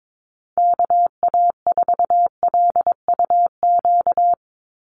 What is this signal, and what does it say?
Morse code, heard as a keyed tone, about 22 wpm, 700 Hz. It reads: KA4LUQ
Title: USA Call Signs 22wpm